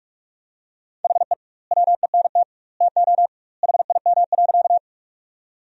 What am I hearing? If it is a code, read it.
Morse code, heard as a keyed tone, about 42 wpm, 700 Hz. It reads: HE WENT TO HIM.
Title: 2x Word-Spacing - Sentences from Top 200 Words - 45wpm